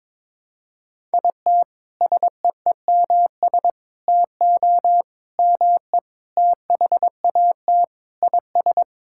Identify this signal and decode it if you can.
Morse code, heard as a keyed tone, about 22 wpm, 700 Hz. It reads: IT SEEMS TO ME THAT IS
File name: practice / 22wpm / Sentences with 4-Word Phrases - 22wpm